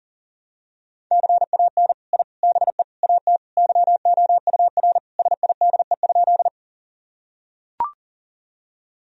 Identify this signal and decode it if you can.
Morse code, heard as a keyed tone, about 40 wpm, 700 Hz. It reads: CAN I BE AT YOUR SIDE?
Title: Sentences from Top 1000 Words - Review Part 1 - 40wpm